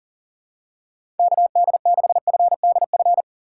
Morse code, heard as a keyed tone, about 40 wpm, 700 Hz. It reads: KD6FDF
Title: USA Call Signs - 6 Characters - 4x Character Spacing - 40wpm